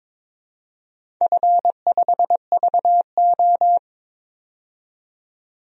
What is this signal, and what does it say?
Morse code, heard as a keyed tone, about 22 wpm, 700 Hz. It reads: F5VO
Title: DXCC-100 Call Signs - 4 Characters - 22wpm